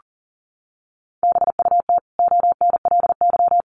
Morse code, heard as a keyed tone, about 40 wpm, 700 Hz. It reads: BUT ONLY
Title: Sentences from Top 1100 Words 40wpm